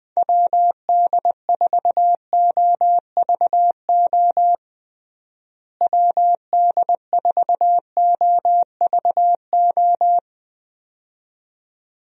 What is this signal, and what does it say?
Morse code, heard as a keyed tone, about 20 wpm, 700 Hz. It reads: WD4OVO WD4OVO
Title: USA Call Signs - 6 Characters - Repeated 2x - 20wpm